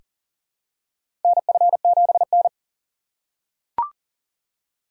Morse code, heard as a keyed tone, about 40 wpm, 700 Hz. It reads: NF7N
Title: USA Call Signs - 4 Characters - 3x Character Spacing - 40wpm